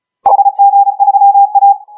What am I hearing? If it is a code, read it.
Morse code, heard as a keyed tone, about 34 wpm, 810 Hz. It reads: HG3A
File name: HG1R CW SSB MIX